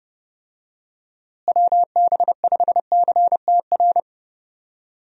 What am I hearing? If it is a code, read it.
Morse code, heard as a keyed tone, about 30 wpm, 700 Hz. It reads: WB5CTR
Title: USA Call Signs - 6 Characters - Repeated 2x - 30wpm